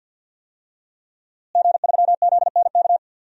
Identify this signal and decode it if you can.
Morse code, heard as a keyed tone, about 51 wpm, 700 Hz. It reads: G3ZNK
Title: DXCC-100 Call Signs - 5 Characters - Repeated 3x - 50wpm